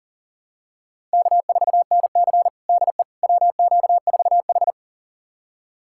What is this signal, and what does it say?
Morse code, heard as a keyed tone, about 40 wpm, 700 Hz. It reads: K4NC DE WQ4H
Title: USA Callsign DE Callsign - 4 Characters - 40wpm